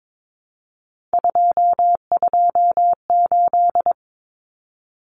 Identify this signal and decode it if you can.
Morse code, heard as a keyed tone, about 22 wpm, 700 Hz. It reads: 228
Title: Common Cut Numbers - 3-Digit 22wpm